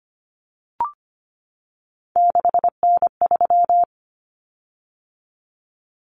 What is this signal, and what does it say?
Morse code, heard as a keyed tone, about 25 wpm, 700 Hz. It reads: 6N3
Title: Common Cut Numbers - 3-Digit 25wpm